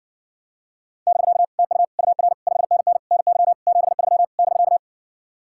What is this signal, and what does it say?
Morse code, heard as a keyed tone, about 50 wpm, 700 Hz. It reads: <BK> TU UR 5NN NY 73 <BK>
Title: Speed-Racing Common QSO Phrases - 40wpm